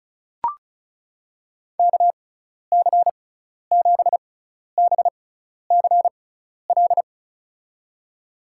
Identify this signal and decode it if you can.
Morse code, heard as a keyed tone, about 34 wpm, spaced as for 12 wpm, 700 Hz. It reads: KC7BCL
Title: USA Call Signs - 6x Character Spacing - 35wpm